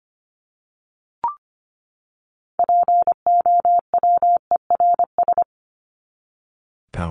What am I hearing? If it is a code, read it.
Morse code, heard as a keyed tone, about 25 wpm, 700 Hz. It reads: POWERS
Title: ES:S Suffix - Top 100 Words with Base Words 25wpm